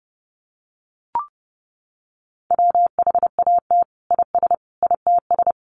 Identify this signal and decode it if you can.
Morse code, heard as a keyed tone, about 28 wpm, 700 Hz. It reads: WHAT IS ITS
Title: Sentences from Top 1100 Words 30wpm